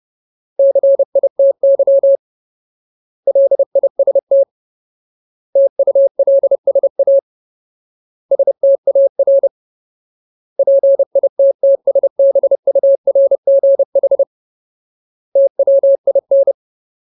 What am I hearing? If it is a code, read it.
Morse code, heard as a keyed tone, about 30 wpm, 550 Hz. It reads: CITY LIST TULSA STAR PITTSBURGH TWIN
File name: Ultimate Word List / 30WPM Ultimate